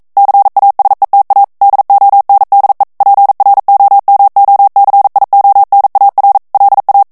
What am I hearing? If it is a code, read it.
Morse code, heard as a keyed tone, about 42 wpm, 800 Hz. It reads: CASETA DONDE PROMOCIONAR LA
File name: morse mp3 / ESPAGNOL TEXTE EN CLAIR 1 VITESSE 40 MOTS MINUTES